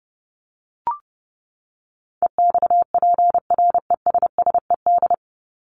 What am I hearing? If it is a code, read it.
Morse code, heard as a keyed tone, about 30 wpm, 700 Hz. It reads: EXPRESSED
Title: ED Suffix - Top 100 Words with Base Words 30wpm